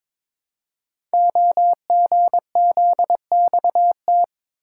Morse code, heard as a keyed tone, about 22 wpm, 700 Hz. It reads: OGZXT
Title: Code Groups - 5 Letters - 22wpm